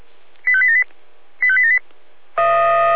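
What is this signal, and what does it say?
Five-tone selective call (ZVEI), heard as quick runs of five tones, tones of about 70 ms. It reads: 85878 85878
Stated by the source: ZVEI five-tone call received by SDR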